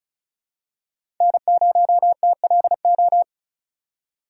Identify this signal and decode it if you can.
Morse code, heard as a keyed tone, about 36 wpm, 700 Hz. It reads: N0TLO
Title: USA Call Signs - 6x Character Spacing - 35wpm